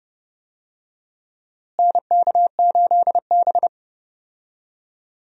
Morse code, heard as a keyed tone, about 30 wpm, 700 Hz. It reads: NK8B
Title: USA Call Signs - 4 Characters - Repeated 2x - 30wpm